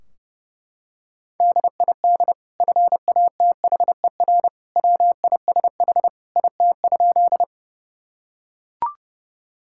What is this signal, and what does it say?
Morse code, heard as a keyed tone, about 30 wpm, 700 Hz. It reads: DID FATHER WISH IT?
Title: Sentences from Top 1600 Words 30wpm